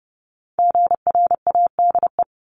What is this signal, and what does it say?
Morse code, heard as a keyed tone, about 30 wpm, 700 Hz. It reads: GRADE
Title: Multi-letter Phonemes 30wpm